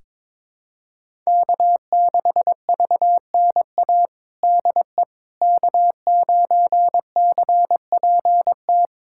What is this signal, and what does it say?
Morse code, heard as a keyed tone, about 21 wpm, 700 Hz. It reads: K6VNA DE K9CPT
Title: USA Callsign DE Callsign - 5 Characters - 22wpm